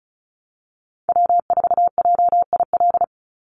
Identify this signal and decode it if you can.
Morse code, heard as a keyed tone, about 36 wpm, 700 Hz. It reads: W4JIL
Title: USA Call Signs 35wpm